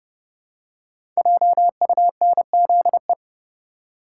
Morse code, heard as a keyed tone, about 30 wpm, 700 Hz. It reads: JUNZE
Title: Code Groups - 5 Letters - 30wpm